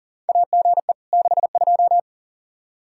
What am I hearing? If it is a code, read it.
Morse code, heard as a keyed tone, about 40 wpm, 700 Hz. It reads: AGE 62
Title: K1USN SST QSO Practice 40wpm